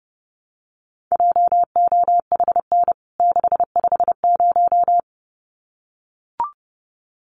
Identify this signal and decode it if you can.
Morse code, heard as a keyed tone, about 30 wpm, 700 Hz. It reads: JOHN 650
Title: CWT Exchange 30wpm